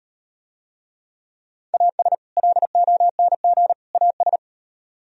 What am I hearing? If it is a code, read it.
Morse code, heard as a keyed tone, about 38 wpm, 700 Hz. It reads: AS LONG AS